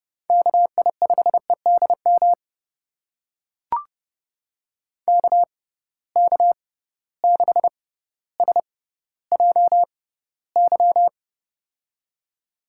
Morse code, heard as a keyed tone, about 30 wpm, spaced as for 15 wpm, 700 Hz. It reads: KI5EDM KK6SJY
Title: USA Call Signs - 6 Characters - 6x Character Spacing - 30wpm